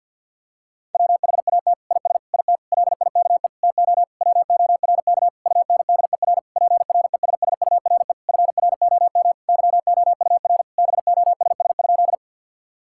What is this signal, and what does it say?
Morse code, heard as a keyed tone, about 52 wpm, 700 Hz. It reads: WHAT IS IT LIKE TO WORK UNDER PRESSURE FROM YOUR BOSS?
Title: Sentences from Top 1000 Words - Review Part 6 - 50wpm